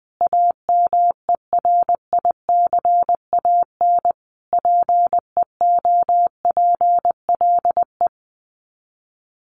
Morse code, heard as a keyed tone, about 19 wpm, 700 Hz. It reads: AMERICAN PEOPLE
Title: Sets of 5 Words Encore 20wpm